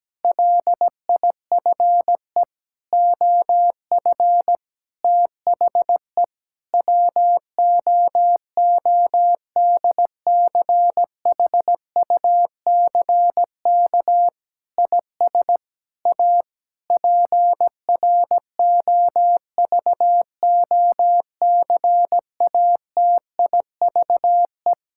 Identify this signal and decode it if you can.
Morse code, heard as a keyed tone, about 17 wpm, 700 Hz. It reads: LIFE OF THE WOODCHUCK IS A PROVOCATIVE
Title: Pangram Challenge v2 - 17wpm